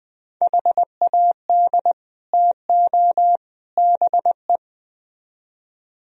Morse code, heard as a keyed tone, about 20 wpm, 700 Hz. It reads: HAD TO BE